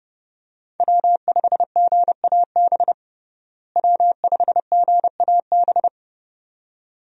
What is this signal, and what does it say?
Morse code, heard as a keyed tone, about 31 wpm, 700 Hz. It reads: W5GAB W5GAB
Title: USA Call Signs - 5 Characters - Repeated 2x - 30wpm